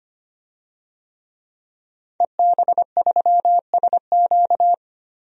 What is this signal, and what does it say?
Morse code, heard as a keyed tone, about 25 wpm, 700 Hz. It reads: EB3SQ
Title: DXCC-100 Call Signs - 5 Characters - 25wpm